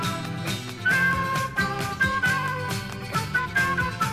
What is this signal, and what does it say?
Emergency Alert System (SAME) tones, heard as harsh buzzing bursts. It is heard from 0.7 to 3.7 s.